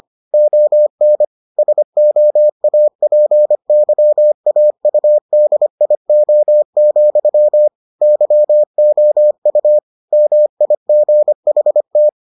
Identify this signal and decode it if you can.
Morse code, heard as a keyed tone, about 25 wpm, 600 Hz. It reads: ON SOAPYAUDIO, YOU MIGHT